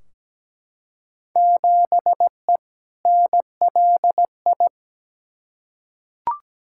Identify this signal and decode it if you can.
Morse code, heard as a keyed tone, about 17 wpm, 700 Hz. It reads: 7E NLI
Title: Field Day Exchanges Encore - 17wpm